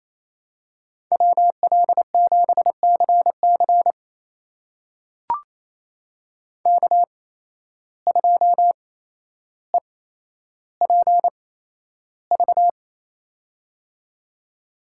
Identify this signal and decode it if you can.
Morse code, heard as a keyed tone, about 28 wpm, spaced as for 12 wpm, 700 Hz. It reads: WL7CC K2EPV
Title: USA Call Signs - 8x Character Spacing - 28wpm